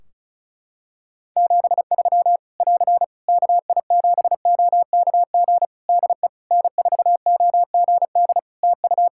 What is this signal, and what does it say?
Morse code, heard as a keyed tone, about 36 wpm, 700 Hz. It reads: 73 <AR> KI7OKG DE N4OGD TU
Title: K1USN SST QSO Practice 35wpm